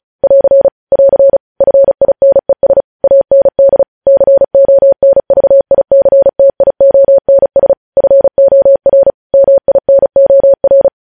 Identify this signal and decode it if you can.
Morse code, heard as a keyed tone, about 36 wpm, 550 Hz. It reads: <AR> <AR> FINES AND CONVICTIONS FOR MINOR